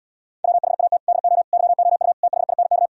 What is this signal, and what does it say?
Morse code, heard as a keyed tone, about 77 wpm, 700 Hz. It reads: WHAT DO YOU THINK